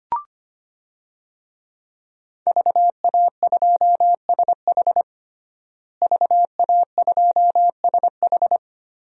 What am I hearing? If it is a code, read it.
Morse code, heard as a keyed tone, about 25 wpm, 700 Hz. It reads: VA2SH VA2SH
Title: DXCC-100 Call Signs - 5 Characters - Repeated 3x - 25wpm